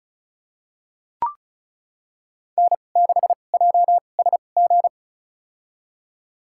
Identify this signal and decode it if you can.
Morse code, heard as a keyed tone, about 34 wpm, spaced as for 25 wpm, 700 Hz. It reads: N6JSG
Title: USA Call Signs - 5 Characters - 2x Character Spacing - 35wpm